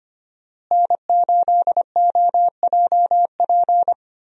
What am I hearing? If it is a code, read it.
Morse code, heard as a keyed tone, about 25 wpm, 700 Hz. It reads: N8OJP
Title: USA Call Signs - 5 Characters - Repeated 3x - 25wpm